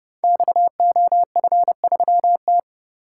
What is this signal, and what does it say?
Morse code, heard as a keyed tone, about 30 wpm, 700 Hz. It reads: XOF3T
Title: Code Groups - 5 Letter:Numbers - 30wpm